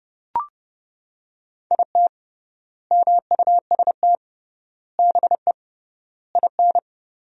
Morse code, heard as a keyed tone, about 29 wpm, 700 Hz. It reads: IT MUST BE IN
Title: 3x Word-Spacing - Sentences from Top 300 Words - 30wpm